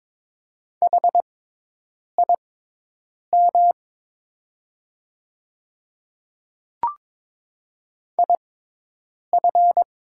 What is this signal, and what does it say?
Morse code, heard as a keyed tone, about 22 wpm, spaced as for 8 wpm, 700 Hz. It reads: HIM IF
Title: 6x Character Spacing - Top 100 Words - 22wpm